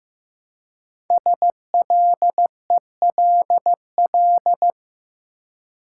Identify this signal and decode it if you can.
Morse code, heard as a keyed tone, about 15 wpm, 700 Hz. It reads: SLELL
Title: Code Groups - 5 Letters - 15wpm